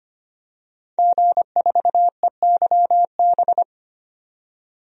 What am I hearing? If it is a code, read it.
Morse code, heard as a keyed tone, about 25 wpm, 690 Hz. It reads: G4EYB
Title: DXCC-100 Call Signs - 5 Characters - 25wpm